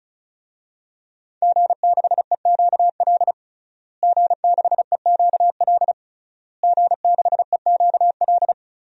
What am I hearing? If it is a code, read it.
Morse code, heard as a keyed tone, about 36 wpm, 700 Hz. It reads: G6EQL G6EQL G6EQL
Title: DXCC-100 Call Signs - 5 Characters - Repeated 3x - 35wpm